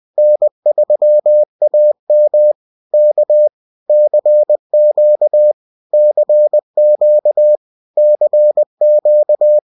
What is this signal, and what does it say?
Morse code, heard as a keyed tone, about 20 wpm, 600 Hz. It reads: N3AM K CQ CQ CQ